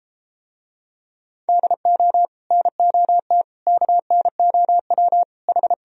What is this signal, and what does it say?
Morse code, heard as a keyed tone, about 33 wpm, 700 Hz. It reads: DO NOT KNOW H